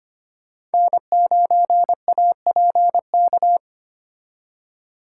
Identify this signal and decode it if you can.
Morse code, heard as a keyed tone, about 32 wpm, 700 Hz. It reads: N9APK